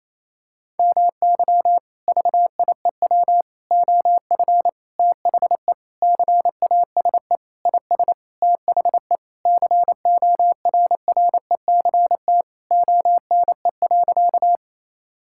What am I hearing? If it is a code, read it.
Morse code, heard as a keyed tone, about 28 wpm, 700 Hz. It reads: MY VIEW OF THE CASE IS THE CORRECT ONE.
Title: Sentences from Top 1000 Words - Review Part 6 - 28wpm